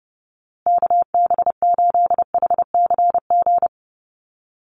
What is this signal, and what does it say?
Morse code, heard as a keyed tone, about 30 wpm, 700 Hz. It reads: KB8HCG
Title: USA Call Signs Repeated 2x 30wpm